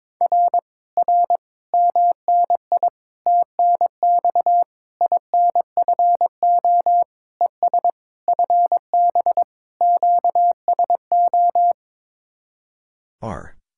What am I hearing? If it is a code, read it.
Morse code, heard as a keyed tone, about 22 wpm, 700 Hz. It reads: R R MNI TNX INFO ES FB QSO
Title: Speed-Racing Common QSO Phrases - 22wpm